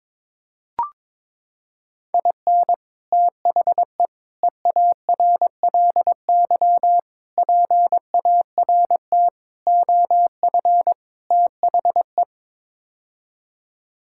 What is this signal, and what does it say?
Morse code, heard as a keyed tone, about 22 wpm, 700 Hz. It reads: IN THE EARLY PART OF THE
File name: practice / 22wpm / Sets of 6 Words - 22wpm